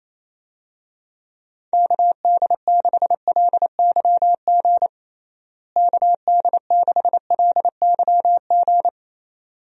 Morse code, heard as a keyed tone, about 28 wpm, 700 Hz. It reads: KD6LYG KD6LYG
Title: USA Call Signs - 6 Characters - Repeated 2x - 28wpm